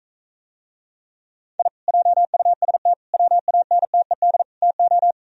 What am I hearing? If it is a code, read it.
Morse code, heard as a keyed tone, about 45 wpm, 700 Hz. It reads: I JUST WANTED TO